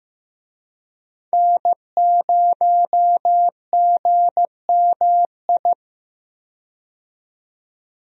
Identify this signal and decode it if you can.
Morse code, heard as a keyed tone, about 15 wpm, 700 Hz. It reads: N0GMI
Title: USA Call Signs - 5 Characters - 15wpm